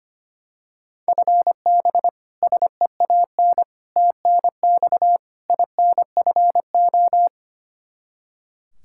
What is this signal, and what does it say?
Morse code, heard as a keyed tone, about 25 wpm, 700 Hz. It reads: FB SEAN TNX INFO
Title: Practice QSO - Exchange 3 - More Info - 25wpm